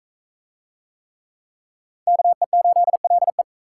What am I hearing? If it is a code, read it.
Morse code, heard as a keyed tone, about 39 wpm, 700 Hz. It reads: KE8LE